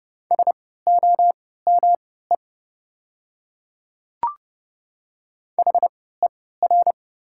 Morse code, heard as a keyed tone, about 30 wpm, spaced as for 17 wpm, 700 Hz. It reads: SOME HER
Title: 3x Character Spacing - Top 100 Words - 30wpm